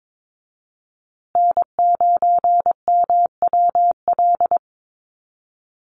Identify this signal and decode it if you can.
Morse code, heard as a keyed tone, about 22 wpm, 700 Hz. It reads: N9MWL
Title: USA Call Signs Repeated 2x 22wpm